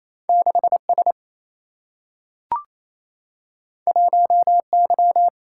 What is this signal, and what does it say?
Morse code, heard as a keyed tone, about 28 wpm, 700 Hz. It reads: 6 S 1 Y
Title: Single-Letter Single-Number 28wpm